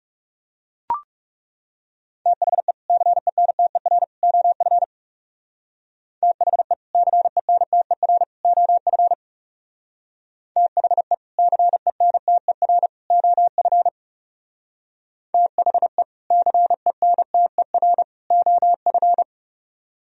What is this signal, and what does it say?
Morse code, heard as a keyed tone, about 36 wpm, 700 Hz. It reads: THE CENTER OF THE CENTER OF THE CENTER OF THE CENTER OF
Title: Speed-Racing - Sets of 3 Words 30wpm